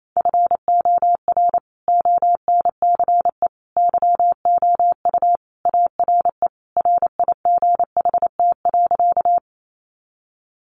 Morse code, heard as a keyed tone, about 28 wpm, 700 Hz. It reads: FOR ONCE YOU ARE RIGHT.
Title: Sentences from Top 300 Words 28wpm